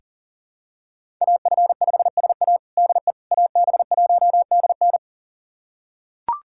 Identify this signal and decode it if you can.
Morse code, heard as a keyed tone, about 40 wpm, 700 Hz. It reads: AF5SA DE AB1DN
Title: USA Callsign DE Callsign - 5 Characters - 40wpm